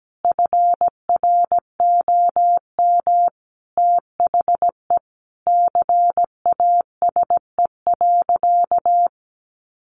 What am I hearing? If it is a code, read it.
Morse code, heard as a keyed tone, about 17 wpm, 700 Hz. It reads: FROM THE CASE.
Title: Sentences from Top 1000 Words 17wpm